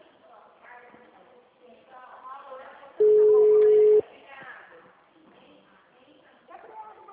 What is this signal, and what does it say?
Ringback tone (Europe, 425 Hz), with speech.